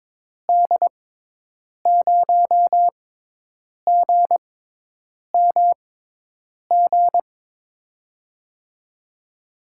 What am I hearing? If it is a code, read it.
Morse code, heard as a keyed tone, about 22 wpm, spaced as for 8 wpm, 700 Hz. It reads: D0GMG
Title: USA Call Signs - 6 Characters - 6x Character Spacing - 22wpm